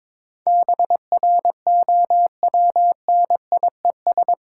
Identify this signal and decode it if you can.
Morse code, heard as a keyed tone, about 22 wpm, 690 Hz. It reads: BROWNIES
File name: Difficult Words with Hints - 22wpm